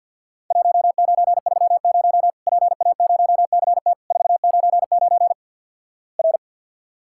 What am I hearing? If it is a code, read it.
Morse code, heard as a keyed tone, about 50 wpm, 700 Hz. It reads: 1930 PA0CT 499 R
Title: Worked All Europe DX Contest - QTC Exchange - 50wpm